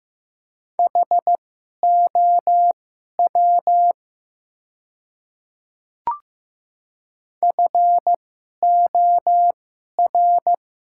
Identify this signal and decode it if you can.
Morse code, heard as a keyed tone, about 15 wpm, spaced as for 11 wpm, 690 Hz. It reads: HOW FOR
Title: 2x Character Spacing - Top 100 Words - 15wpm